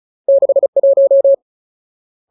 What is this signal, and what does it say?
Morse code, heard as a keyed tone, about 34 wpm, 550 Hz. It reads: 61
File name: Ultimate Word List / 35WPM Ultimate